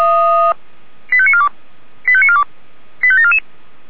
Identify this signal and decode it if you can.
Five-tone selective call (ZVEI), heard as quick runs of five tones, tones of about 70 ms. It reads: 85942 85942 85744